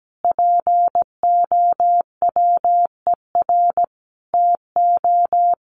Morse code, heard as a keyed tone, about 17 wpm, 700 Hz. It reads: POWER TO
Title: Sentences from Top 1400 Words 17wpm